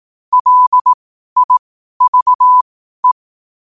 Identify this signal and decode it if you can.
Morse code, heard as a keyed tone, about 18 wpm, spaced as for 12 wpm, 1000 Hz. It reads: LIVE
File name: qotdfiles / 12wpm